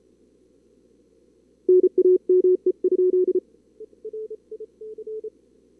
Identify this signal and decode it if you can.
Morse code, heard as a keyed tone, about 33 wpm, 370 Hz. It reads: NAME?
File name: CWSprint / Feb2022 / k5tr-feb-na-sprint-cw-k5ot -2300